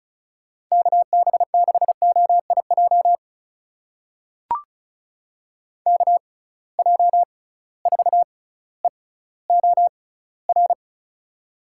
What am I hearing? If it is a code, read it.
Morse code, heard as a keyed tone, about 36 wpm, spaced as for 18 wpm, 690 Hz. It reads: KB6OIJ KJ4EOR
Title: USA Call Signs - 6 Characters - 6x Character Spacing - 35wpm